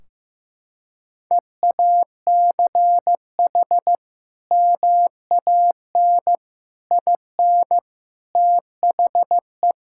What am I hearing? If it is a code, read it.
Morse code, heard as a keyed tone, about 15 wpm, 700 Hz. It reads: EACH MAN IN THE
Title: Sentences from Top 1600 Words 15wpm